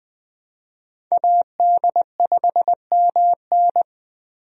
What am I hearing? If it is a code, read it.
Morse code, heard as a keyed tone, about 21 wpm, 700 Hz. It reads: AD5MN
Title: USA Call Signs - 5 Characters - Repeated 3x - 20wpm